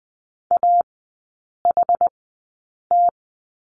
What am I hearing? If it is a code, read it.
Morse code, heard as a keyed tone, about 20 wpm, spaced as for 8 wpm, 700 Hz. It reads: AHT
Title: 133 Lesson 22e - Question Mark - Three Characters 20wpm